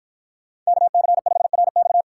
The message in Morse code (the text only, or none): KC5RX